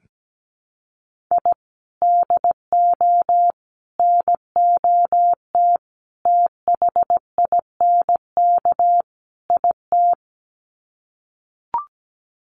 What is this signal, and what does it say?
Morse code, heard as a keyed tone, about 17 wpm, 700 Hz. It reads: I DO NOT THINK IT
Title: Sets of 5 Words 17wpm